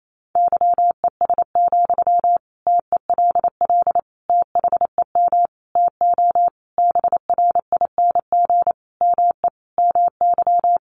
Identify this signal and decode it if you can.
Morse code, heard as a keyed tone, about 28 wpm, 700 Hz. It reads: YES, TELL THEM TO BRING ME MY
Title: Sentences from Top 500 Words 28wpm